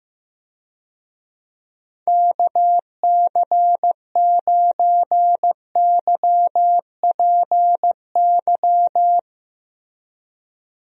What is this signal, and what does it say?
Morse code, heard as a keyed tone, about 15 wpm, 700 Hz. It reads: KC9YPY
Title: USA Call Signs - 6 Characters - Repeated 2x - 15wpm